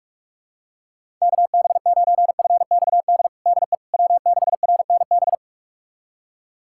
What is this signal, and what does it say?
Morse code, heard as a keyed tone, about 46 wpm, 690 Hz. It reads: KB9FXD DE W6RNB
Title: USA Callsign DE Callsign 45wpm